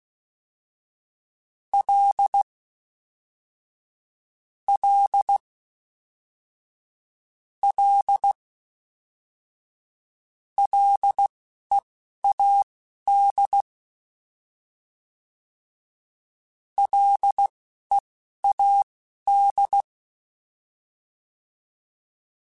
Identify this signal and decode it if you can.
Morse code, heard as a keyed tone, about 16 wpm, spaced as for 11 wpm, 800 Hz. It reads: L L L LEAD LEAD